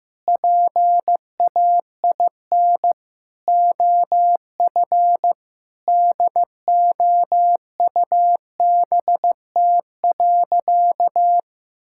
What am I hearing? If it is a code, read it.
Morse code, heard as a keyed tone, about 15 wpm, 700 Hz. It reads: PAIN OF DOUBT.
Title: Sentences from Top 1600 Words 15wpm